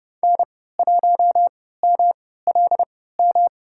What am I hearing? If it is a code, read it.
Morse code, heard as a keyed tone, about 30 wpm, spaced as for 17 wpm, 700 Hz. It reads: N1MLM
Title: USA Call Signs - 5 Characters - 3x Character Spacing - 30wpm